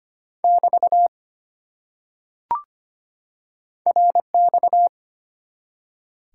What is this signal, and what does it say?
Morse code, heard as a keyed tone, about 25 wpm, 700 Hz. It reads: <BT> RX
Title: Top 90 QSO Elements 25wpm